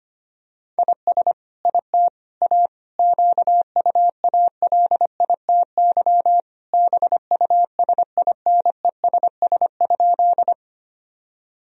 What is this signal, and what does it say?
Morse code, heard as a keyed tone, about 25 wpm, 700 Hz. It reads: IS IT A QUALITY BUSINESS?
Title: Sentences from Top 1000 Words - Review Part 4 - 25wpm